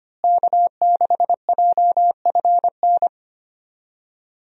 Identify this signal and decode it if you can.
Morse code, heard as a keyed tone, about 25 wpm, 700 Hz. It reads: K6JFN
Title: USA Call Signs - 5 Characters - Repeated 2x - 25wpm